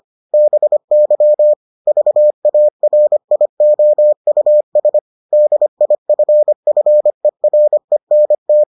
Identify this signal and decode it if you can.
Morse code, heard as a keyed tone, about 25 wpm, 600 Hz. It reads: BY VARIOUS DIFFERENT